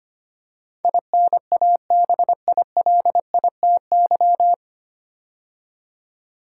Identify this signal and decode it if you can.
Morse code, heard as a keyed tone, about 25 wpm, 700 Hz. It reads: INABILITY
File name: practice / 25wpm / 9 Letter Words - First 3 Letters Given - 25wpm